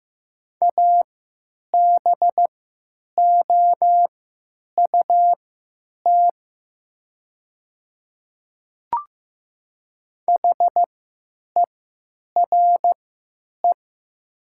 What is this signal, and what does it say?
Morse code, heard as a keyed tone, about 15 wpm, spaced as for 9 wpm, 700 Hz. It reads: ABOUT HERE
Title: 3x Character Spacing - Top 300 Words - 15wpm